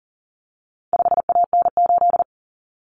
Morse code, heard as a keyed tone, about 40 wpm, 700 Hz. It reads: 5AN8
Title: Common Cut Numbers - 4-Digit 40wpm